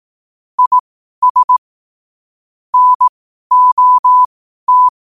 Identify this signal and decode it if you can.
Morse code, heard as a keyed tone, about 18 wpm, spaced as for 12 wpm, 1000 Hz. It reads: IS NOT
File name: qotdfiles / 12wpm